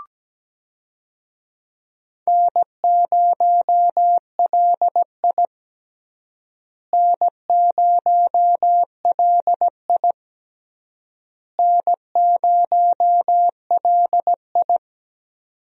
Morse code, heard as a keyed tone, about 17 wpm, 700 Hz. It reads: N0LI N0LI N0LI
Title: USA Call Signs - 4 Characters - Repeated 3x - 17wpm